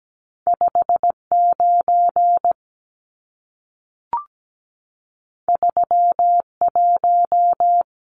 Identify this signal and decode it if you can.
Morse code, heard as a keyed tone, about 17 wpm, 700 Hz. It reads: 59 31
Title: Numbers - 2-Digit 17wpm